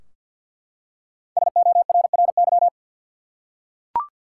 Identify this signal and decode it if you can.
Morse code, heard as a keyed tone, about 52 wpm, 700 Hz. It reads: SORRY
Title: Five Letter Words - 50wpm